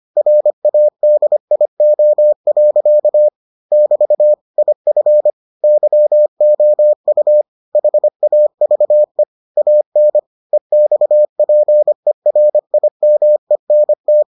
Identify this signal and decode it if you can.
Morse code, heard as a keyed tone, about 25 wpm, 600 Hz. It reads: RADIO. <BT> IF YOU HAVE AN EXPERIMENT